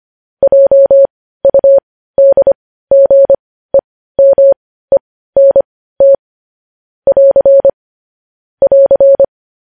Morse code, heard as a keyed tone, about 25 wpm, spaced as for 15 wpm, 550 Hz. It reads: JUDGEMENT <AR> <AR>
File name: podcast / News headlines-1x-25x15